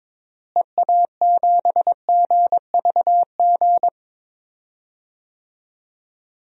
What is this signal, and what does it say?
Morse code, heard as a keyed tone, about 22 wpm, 700 Hz. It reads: EA7GVG
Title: DXCC-100 Call Signs - 6 Characters - Repeated 3x - 22wpm